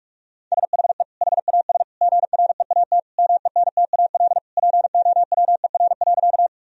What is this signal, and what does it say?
Morse code, heard as a keyed tone, about 41 wpm, 700 Hz. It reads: SHE HAS GREAT MENTAL POWER.
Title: Sentences from Top 1000 Words - Review Part 4 - 45wpm